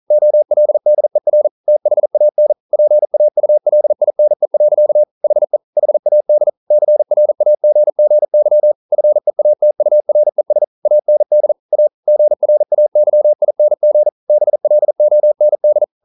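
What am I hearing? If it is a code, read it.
Morse code, heard as a keyed tone, about 41 wpm, 600 Hz. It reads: OLDER THAN PAULINE. HE HAD CRAGGY FEATURES AND A GRAYING BLOND
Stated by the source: Morse code practice at 41 wpm, from CW med Gnister.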